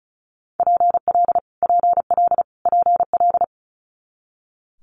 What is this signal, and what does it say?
Morse code, heard as a keyed tone, about 36 wpm, 700 Hz. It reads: PL PL PL
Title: Multi-letter Phonemes Repeated 3x 35wpm